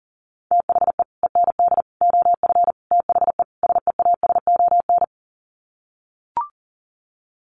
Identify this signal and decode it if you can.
Morse code, heard as a keyed tone, about 40 wpm, 700 Hz. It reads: THE END OF THE SEASON
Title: Sets of 5 Words Encore 40wpm